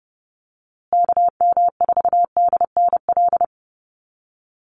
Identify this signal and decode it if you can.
Morse code, heard as a keyed tone, about 30 wpm, 700 Hz. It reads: KM4DNL
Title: USA Call Signs Repeated 3x 30wpm